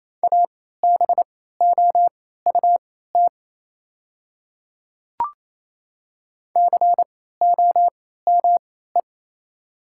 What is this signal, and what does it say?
Morse code, heard as a keyed tone, about 28 wpm, spaced as for 16 wpm, 700 Hz. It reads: ABOUT COME
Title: 3x Character Spacing - Top 300 Words - 28wpm